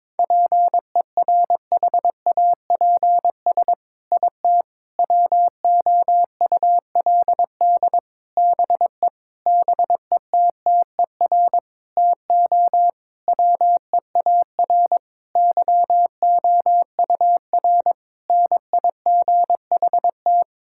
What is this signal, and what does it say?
Morse code, heard as a keyed tone, about 22 wpm, 700 Hz. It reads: PERHAPS IT WOULD BE BETTER TO WEAR YOUR NIGHT
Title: Sentences from Top 1600 Words 22wpm